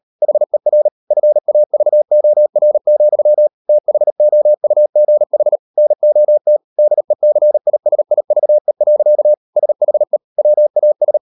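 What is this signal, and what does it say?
Morse code, heard as a keyed tone, about 38 wpm, 600 Hz. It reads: HER FAVOR, THOUGH NOT DECISIVE. SHE WAS
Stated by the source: Morse code practice at 38 WPM, sending Ken Follett's novel Never in English, from CW med Gnister.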